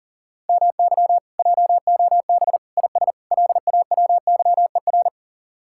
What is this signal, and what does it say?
Morse code, heard as a keyed tone, about 53 wpm, 700 Hz. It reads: MY JOB IS LAWYER